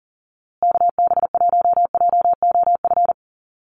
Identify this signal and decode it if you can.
Morse code, heard as a keyed tone, about 40 wpm, 700 Hz. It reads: KB1JOF
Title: USA Call Signs Repeated 2x 40wpm